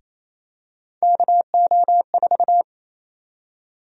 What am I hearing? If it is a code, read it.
Morse code, heard as a keyed tone, about 28 wpm, 700 Hz. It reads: KO4
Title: USA Call Sign Prefixes - 28wpm